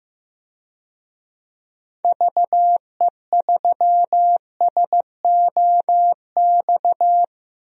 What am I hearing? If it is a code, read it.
Morse code, heard as a keyed tone, about 15 wpm, 700 Hz. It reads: VE3SOX
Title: DXCC-100 Call Signs - 6 Characters - Repeated 3x - 15wpm